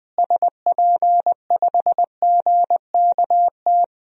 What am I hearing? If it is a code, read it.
Morse code, heard as a keyed tone, about 20 wpm, 700 Hz. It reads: SP5GKT
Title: DXCC-100 Call Signs - 6 Characters - Repeated 3x - 20wpm